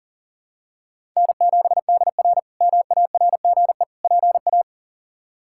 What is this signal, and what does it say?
Morse code, heard as a keyed tone, about 40 wpm, 700 Hz. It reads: N7DR MARGE PA
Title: K1USN SST QSO Practice - 40wpm